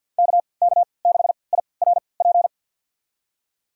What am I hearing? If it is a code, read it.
Morse code, heard as a keyed tone, about 52 wpm, spaced as for 29 wpm, 700 Hz. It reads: KK6IRP
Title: USA Call Signs - 6 Characters - 3x Character Spacing - 50wpm